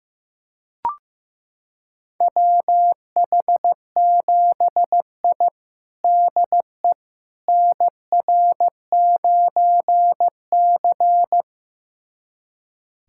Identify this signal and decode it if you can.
Morse code, heard as a keyed tone, about 15 wpm, 700 Hz. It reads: WH7I DE NR9C
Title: USA Callsign DE Callsign - 4 Characters - 15wpm